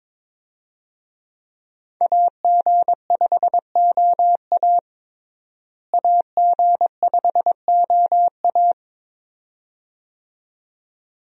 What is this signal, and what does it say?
Morse code, heard as a keyed tone, about 21 wpm, 700 Hz. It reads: AG5OA AG5OA
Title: USA Call Signs - 5 Characters - Repeated 2x - 22wpm